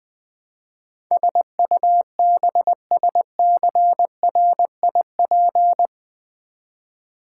Morse code, heard as a keyed tone, about 20 wpm, 700 Hz. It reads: SUBSCRIP
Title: Build A Word - 4 Letters At A Time - 12-Letter Words - 20wpm